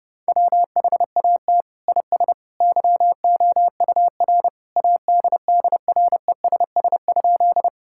Morse code, heard as a keyed tone, about 30 wpm, 700 Hz. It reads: WHAT IS YOUR ADDRESS?
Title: Sentences from Top 1000 Words - Review Part 6 - 30wpm